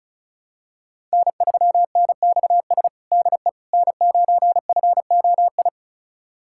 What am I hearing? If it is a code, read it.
Morse code, heard as a keyed tone, about 34 wpm, 700 Hz. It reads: N3NXS DE N9FOI
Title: USA Callsign DE Callsign - 5 Characters - 35wpm